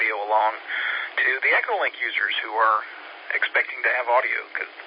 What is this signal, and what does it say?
Emergency Alert System (SAME) tones, heard as harsh buzzing bursts, with speech. It is heard from 0.6 to 3.8 s.